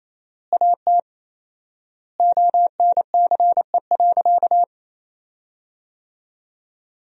Morse code, heard as a keyed tone, about 28 wpm, 700 Hz. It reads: AT ONCE.
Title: 4x Word-Spacing - Sentences from Top 500 Words - 28wpm